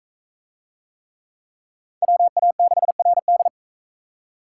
Morse code, heard as a keyed tone, about 42 wpm, 700 Hz. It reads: WA6RD